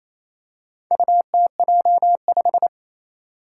Morse code, heard as a keyed tone, about 28 wpm, 700 Hz. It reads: UTJ5
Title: Four Challenging Letter-Number Combinations 28wpm